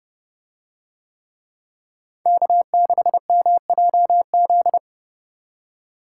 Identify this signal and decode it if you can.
Morse code, heard as a keyed tone, about 30 wpm, 700 Hz. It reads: K6MJZ